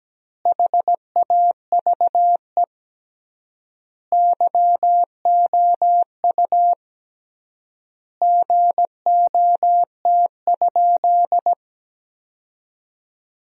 Morse code, heard as a keyed tone, about 17 wpm, 700 Hz. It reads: HAVE YOU GOT?
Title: 3x Word-Spacing - Sentences from Top 300 Words - 17wpm